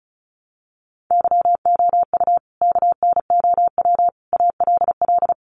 Morse code, heard as a keyed tone, about 34 wpm, 700 Hz. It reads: YOU KNOW ALL
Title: Sentences from Top 300 Words 35wpm